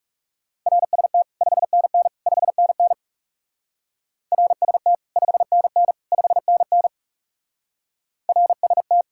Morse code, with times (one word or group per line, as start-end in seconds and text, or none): 0.7-1.2 RST
1.4-2.1 5NN
2.3-2.9 5NN
4.3-4.9 RST
5.2-5.9 5NN
6.1-6.9 5NN
8.3-9.0 RST